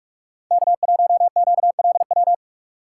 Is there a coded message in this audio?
Yes: Morse code from 0.5 to 2.3 s, heard as a keyed tone, about 44 wpm, 700 Hz. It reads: K1QLW